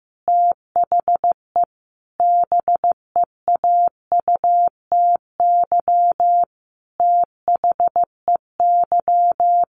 Morse code, heard as a keyed tone, about 15 wpm, 700 Hz. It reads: THE BEAUTY THEY
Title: Sentences from Top 1400 Words 15wpm